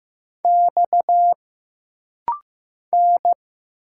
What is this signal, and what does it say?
Morse code, heard as a keyed tone, about 15 wpm, 700 Hz. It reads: X N
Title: Single Letters - Rapid-Fire 15wpm